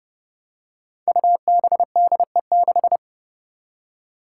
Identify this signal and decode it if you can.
Morse code, heard as a keyed tone, about 32 wpm, 700 Hz. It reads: UBDE6
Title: Five Challenging Letter-Number Combinations 30wpm